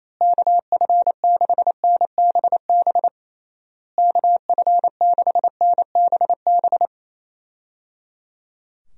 Morse code, heard as a keyed tone, about 28 wpm, 700 Hz. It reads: KF6NBB KF6NBB
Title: USA Call Signs - 6 Characters - Repeated 3x - 28wpm